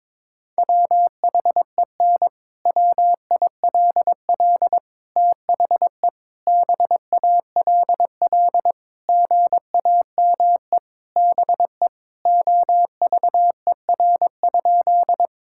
Morse code, heard as a keyed tone, about 22 wpm, 700 Hz. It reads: WHEN WILL THE BALL GAME BE OVER?
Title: Sentences from Top 1000 Words - Review Part 6 - 22wpm